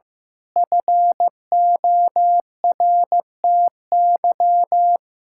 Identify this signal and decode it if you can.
Morse code, heard as a keyed tone, about 15 wpm, 700 Hz. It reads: FORTY
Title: Pangram Challenge v2 - 15wpm